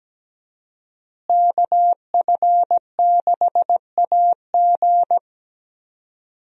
Morse code, heard as a keyed tone, about 17 wpm, 700 Hz. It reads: KF6AG
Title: USA Call Signs - 5 Characters - Repeated 2x - 17wpm